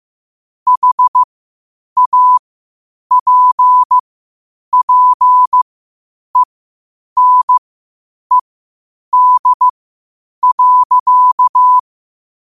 Morse code, heard as a keyed tone, about 15 wpm, spaced as for 8 wpm, 1000 Hz. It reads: HAPPENED.
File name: qotdfiles / 8wpm